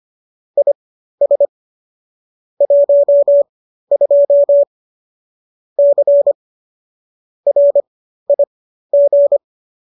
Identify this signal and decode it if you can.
Morse code, heard as a keyed tone, about 25 wpm, spaced as for 13 wpm, 570 Hz. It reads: IS 12 C RIG